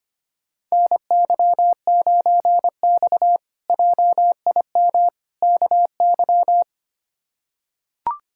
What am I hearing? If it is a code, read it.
Morse code, heard as a keyed tone, about 25 wpm, 700 Hz. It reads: NY9X JIM KY
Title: K1USN SST QSO Practice - 25wpm